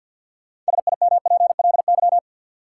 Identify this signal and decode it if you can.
Morse code, heard as a keyed tone, about 50 wpm, 700 Hz. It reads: SIMPLY
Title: Sentences from Top 600 Words - New Word Drill - 50wpm